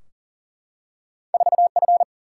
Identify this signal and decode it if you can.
Morse code, heard as a keyed tone, about 40 wpm, 700 Hz. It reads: 4F
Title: Single-Letter Single-Number 40wpm